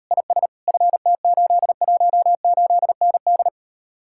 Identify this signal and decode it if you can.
Morse code, heard as a keyed tone, about 38 wpm, 700 Hz. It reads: IS FT818ND